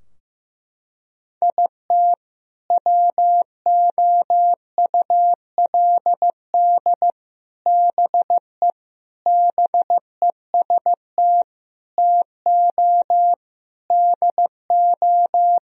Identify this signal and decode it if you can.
Morse code, heard as a keyed tone, about 16 wpm, 700 Hz. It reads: IT WOULD BE BEST TO DO
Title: Sentences from Top 1000 Words - Review Part 2 - 15wpm